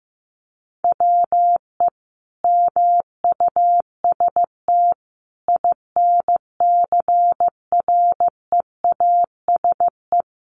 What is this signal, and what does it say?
Morse code, heard as a keyed tone, about 15 wpm, 700 Hz. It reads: WE MUST INCREASE
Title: Sentences from Top 1300 Words 15wpm